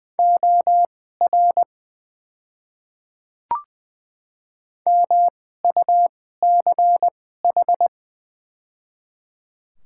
Morse code, heard as a keyed tone, about 20 wpm, spaced as for 15 wpm, 700 Hz. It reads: OR MUCH
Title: 2x Character Spacing - Top 300 Words - 20wpm